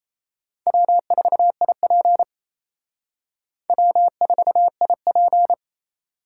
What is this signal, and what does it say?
Morse code, heard as a keyed tone, about 35 wpm, 700 Hz. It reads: W4IP W4IP
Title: Speed Racing - USA 1x2 Call Signs 28wpm